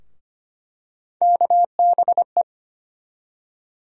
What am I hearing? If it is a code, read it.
Morse code, heard as a keyed tone, about 25 wpm, 700 Hz. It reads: KBE
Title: USA Call Sign Postfixes - 3 Letters 25wpm